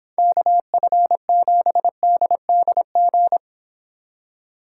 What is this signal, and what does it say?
Morse code, heard as a keyed tone, about 29 wpm, 700 Hz. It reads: KF7DDG